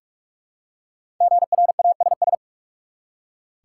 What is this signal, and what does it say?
Morse code, heard as a keyed tone, about 46 wpm, 700 Hz. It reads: GRASS
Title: Five Letter Words - 45wpm